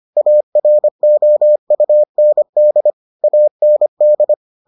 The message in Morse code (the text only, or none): AROUND AND